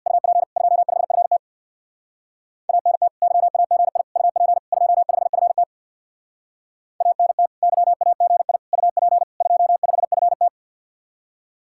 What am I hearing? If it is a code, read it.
Morse code, heard as a keyed tone, about 57 wpm, 700 Hz. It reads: UP 25FT ANT YAGI UP 25FT ANT YAGI UP 25FT
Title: Speed-Racing Common QSO Phrases - 50wpm